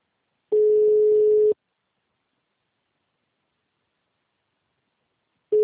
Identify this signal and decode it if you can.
Ringback tone (Europe, 425 Hz).